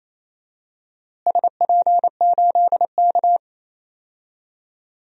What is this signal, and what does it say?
Morse code, heard as a keyed tone, about 28 wpm, 700 Hz. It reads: SP8K
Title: DXCC-100 Call Signs - 4 Characters - Repeated 3x - 28wpm